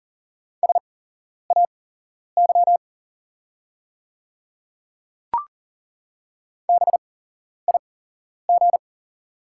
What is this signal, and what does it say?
Morse code, heard as a keyed tone, about 40 wpm, spaced as for 11 wpm, 690 Hz. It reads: SAY BIG
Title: 8x Character Spacing - Top 200 Words - 40wpm